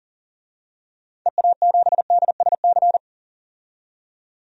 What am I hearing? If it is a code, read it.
Morse code, heard as a keyed tone, about 40 wpm, 700 Hz. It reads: EA7DSC
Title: DXCC-100 Call Signs - 6 Characters - 40wpm